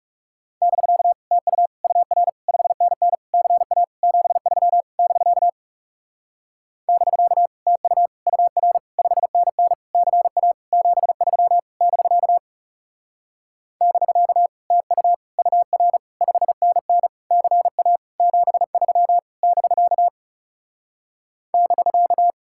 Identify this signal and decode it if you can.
Morse code, heard as a keyed tone, about 40 wpm, 700 Hz. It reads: <BK> TU UR 5NN CA 73 <BK> <BK> TU UR 5NN CA 73 <BK> <BK> TU UR 5NN CA 73 <BK> <BK>